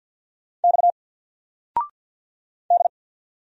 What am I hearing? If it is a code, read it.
Morse code, heard as a keyed tone, about 50 wpm, 700 Hz. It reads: X D
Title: Single Letter-Number 50wpm